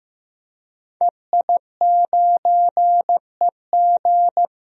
Morse code, heard as a keyed tone, about 15 wpm, 700 Hz. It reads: EI9EG
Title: DXCC-100 Call Signs - 5 Characters - Repeated 3x - 15wpm